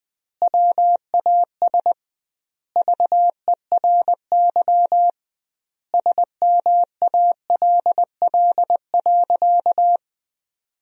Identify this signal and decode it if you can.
Morse code, heard as a keyed tone, about 20 wpm, 700 Hz. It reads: WAS VERY SMALL.
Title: 2x Word-Spacing - Sentences from Top 400 Words - 20wpm